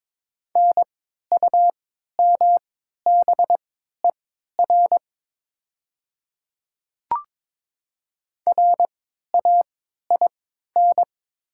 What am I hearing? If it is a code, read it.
Morse code, heard as a keyed tone, about 22 wpm, spaced as for 13 wpm, 690 Hz. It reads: NUMBER RAIN
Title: 3x Character Spacing - Top 300 Words - 22wpm